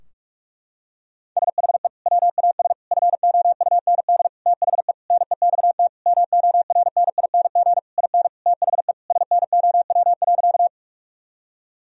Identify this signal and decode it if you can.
Morse code, heard as a keyed tone, about 46 wpm, 700 Hz. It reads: SHE WAS FOUND THE NEXT MORNING IN THE SNOW.
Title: Sentences from Top 1000 Words - Review Part 3 - 45wpm